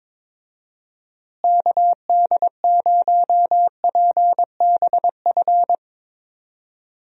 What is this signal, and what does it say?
Morse code, heard as a keyed tone, about 22 wpm, 700 Hz. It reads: KD0PBF
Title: USA Call Signs - 6 Characters - Repeated 3x - 22wpm